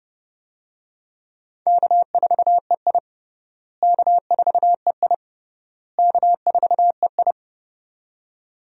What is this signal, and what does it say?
Morse code, heard as a keyed tone, about 32 wpm, 700 Hz. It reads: K4EI K4EI K4EI
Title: USA Call Signs - 4 Characters - Repeated 3x - 30wpm